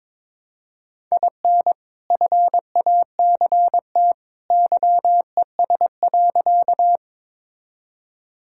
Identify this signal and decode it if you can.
Morse code, heard as a keyed tone, about 22 wpm, 700 Hz. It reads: IN FACT YES.
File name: practice / 22wpm / Sentences with 2-Word Phrases - 22wpm